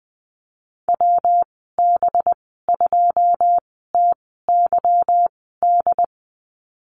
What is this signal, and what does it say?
Morse code, heard as a keyed tone, about 20 wpm, spaced as for 15 wpm, 700 Hz. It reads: WB2TYD
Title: 193 Lesson 31g - Letter v - Call Signs 20wpm